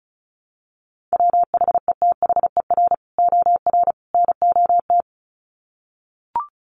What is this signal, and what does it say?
Morse code, heard as a keyed tone, about 34 wpm, 700 Hz. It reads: WHETHER OR NOT
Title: Sets of 3 Words 35wpm